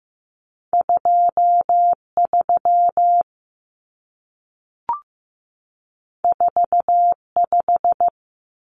Morse code, heard as a keyed tone, about 15 wpm, 700 Hz. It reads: 23 45
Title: Numbers - 2-Digit 15wpm